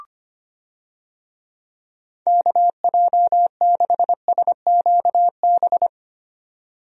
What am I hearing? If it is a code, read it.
Morse code, heard as a keyed tone, about 25 wpm, 700 Hz. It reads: KJ6SQB
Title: USA Call Signs - 6 Characters - 25wpm